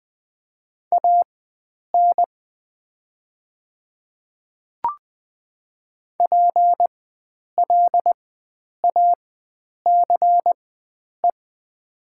Morse code, heard as a keyed tone, about 20 wpm, spaced as for 9 wpm, 700 Hz. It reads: AN PLACE
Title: 4x Character Spacing - Top 300 Words - 20wpm